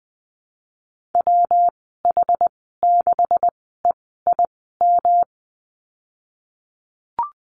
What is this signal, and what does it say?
Morse code, heard as a keyed tone, about 20 wpm, spaced as for 15 wpm, 700 Hz. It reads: WH6EIM
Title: 179 Lesson 29g - Slash - Call Signs 20wpm